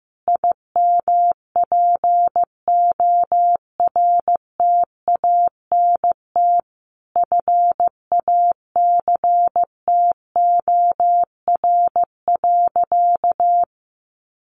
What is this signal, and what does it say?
Morse code, heard as a keyed tone, about 15 wpm, 700 Hz. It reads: IMPORTANT FACTOR.
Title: Sentences from Top 1300 Words 15wpm